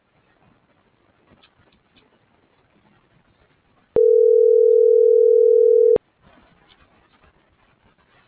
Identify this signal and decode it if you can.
Ringback tone (North America), 440 + 480 Hz.